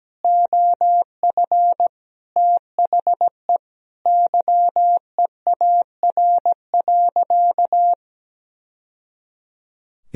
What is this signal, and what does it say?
Morse code, heard as a keyed tone, about 17 wpm, 700 Hz. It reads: OF THE YEAR.
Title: Sentences from Top 1000 Words - Review Part 1 - 17wpm